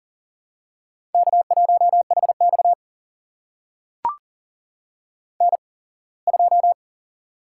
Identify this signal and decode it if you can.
Morse code, heard as a keyed tone, about 40 wpm, spaced as for 24 wpm, 700 Hz. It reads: K1HX N2
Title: USA Call Signs - 8x Character Spacing - 40wpm